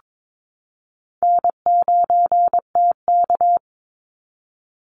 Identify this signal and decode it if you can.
Morse code, heard as a keyed tone, about 22 wpm, 700 Hz. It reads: N9TK
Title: Easy Call Signs Repeated 3x - 22wpm